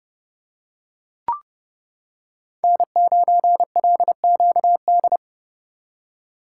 Morse code, heard as a keyed tone, about 31 wpm, 700 Hz. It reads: N9LQD